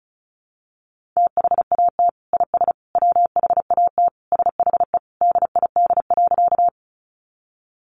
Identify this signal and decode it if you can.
Morse code, heard as a keyed tone, about 34 wpm, 700 Hz. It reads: THAT IS WHAT SHE DID.
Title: Sentences from Top 100 Words 35wpm